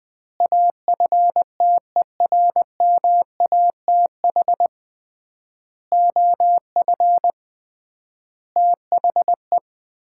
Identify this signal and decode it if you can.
Morse code, heard as a keyed tone, about 20 wpm, 700 Hz. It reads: AFTERMATH OF THE
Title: 3x Word-Spacing - Sets of 5 Words - 20wpm